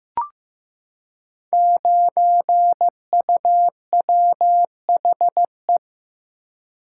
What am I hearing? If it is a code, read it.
Morse code, heard as a keyed tone, about 15 wpm, 700 Hz. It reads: 9UWHE
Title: Code Groups - 5 Letter:Numbers - 15wpm